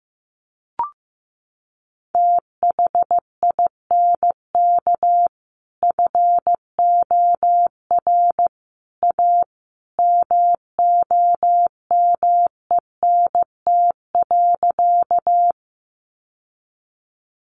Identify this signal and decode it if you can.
Morse code, heard as a keyed tone, about 15 wpm, 700 Hz. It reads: THINK FOR A MOMENT.
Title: Sentences from Top 1000 Words 15wpm